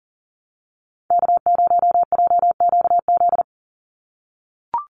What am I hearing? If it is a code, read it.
Morse code, heard as a keyed tone, about 40 wpm, 700 Hz. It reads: K0JQZ
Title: NA SOTA Chaser Callsigns 40wpm